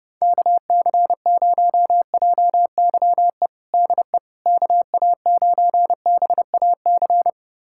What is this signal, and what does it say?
Morse code, heard as a keyed tone, about 30 wpm, 700 Hz. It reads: KC0JYE DE KA9BAC
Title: USA Callsign DE Callsign - 6 Characters - 30wpm